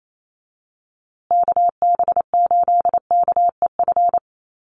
Morse code, heard as a keyed tone, about 28 wpm, 700 Hz. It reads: KB8KEF
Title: USA Call Signs Repeated 3x 28wpm